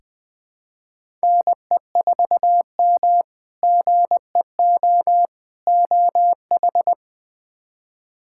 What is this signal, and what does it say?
Morse code, heard as a keyed tone, about 20 wpm, 700 Hz. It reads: NE4M GEO OH
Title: K1USN SST QSO Practice - 20wpm